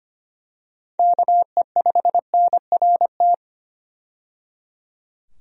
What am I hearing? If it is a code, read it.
Morse code, heard as a keyed tone, about 25 wpm, 700 Hz. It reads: KE5NRT
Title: USA Call Signs - 6 Characters - 25wpm